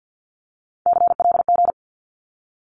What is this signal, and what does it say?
Morse code, heard as a keyed tone, about 50 wpm, 700 Hz. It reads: CLG
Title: Abbreviations 50wpm